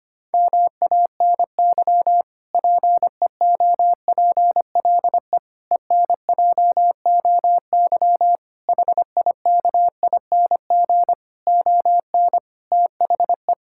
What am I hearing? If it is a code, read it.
Morse code, heard as a keyed tone, about 25 wpm, 700 Hz. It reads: MANY PEOPLE ENJOY HIKING ON THE
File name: practice / 25wpm / Difficult Sentences with Hints - 25wpm